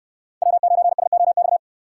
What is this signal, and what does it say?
Morse code, heard as a keyed tone, about 62 wpm, 700 Hz. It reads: W9SOX